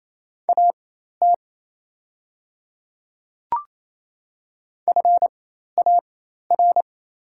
Morse code, heard as a keyed tone, about 27 wpm, spaced as for 13 wpm, 690 Hz. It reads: AT FAR
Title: 4x Character Spacing - Top 500 Words - 28wpm